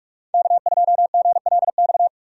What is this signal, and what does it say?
Morse code, heard as a keyed tone, about 47 wpm, 700 Hz. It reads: K2GLX
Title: USA Call Signs - 5 Characters - 6x Character Spacing - 45wpm